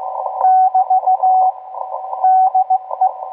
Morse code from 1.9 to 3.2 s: DE